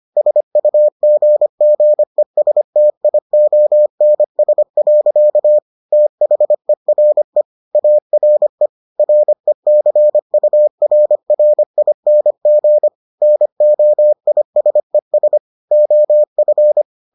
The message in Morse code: SUGGESTIONS. THERE ARE RECURRING NOISES OF

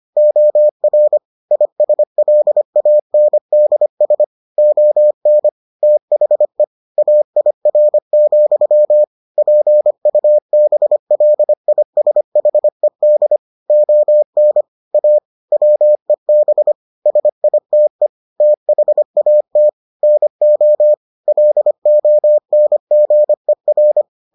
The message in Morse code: OR ISLANDS ON THE AIR, PUBLISHED ON A WEB SITE THAT NO LONGER

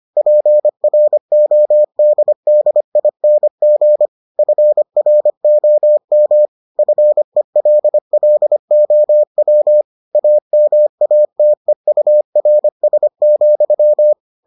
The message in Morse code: PRODDING FROM FELLOW AMATEURS,

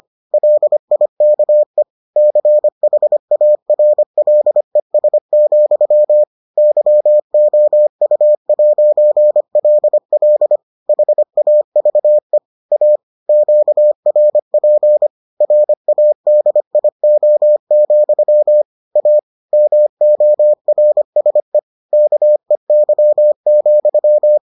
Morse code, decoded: LIKE CHARLES, YOU'LL HAVE A QRP RADIO, A MORSE KEY,

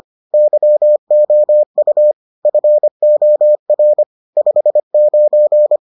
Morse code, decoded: YOU FOR 59

